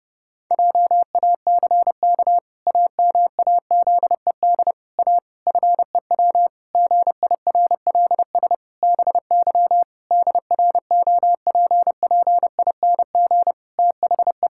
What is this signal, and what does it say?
Morse code, heard as a keyed tone, about 32 wpm, 700 Hz. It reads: JACK AMAZED A FEW GIRLS BY DROPPING THE
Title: Pangram Challenge v2 - 30wpm